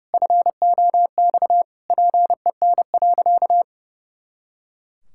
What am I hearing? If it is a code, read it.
Morse code, heard as a keyed tone, about 30 wpm, 700 Hz. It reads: FOX PEN.
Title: Pangram Challenge v2 - 30wpm